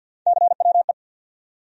Morse code, heard as a keyed tone, about 47 wpm, 700 Hz. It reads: CWE
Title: USA Call Sign Postfixes - 3 Letters 50wpm